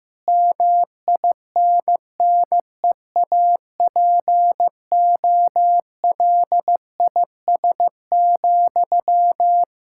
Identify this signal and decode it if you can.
Morse code, heard as a keyed tone, about 15 wpm, 700 Hz. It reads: MINNEAPOLIS,
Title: Practice QSO - Exchange 1 - Beginning - 15wpm